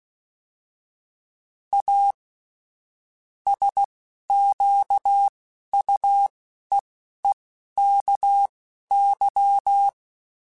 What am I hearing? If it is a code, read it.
Morse code, heard as a keyed tone, about 16 wpm, spaced as for 12 wpm, 800 Hz. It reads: A SQUEEKY